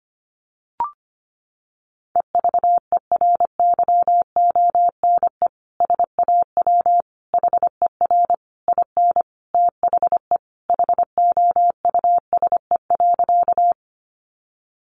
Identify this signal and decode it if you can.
Morse code, heard as a keyed tone, about 25 wpm, 700 Hz. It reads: EVERYONE SAW HER IN THE HOUSE.
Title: Sentences from Top 800 Words 25wpm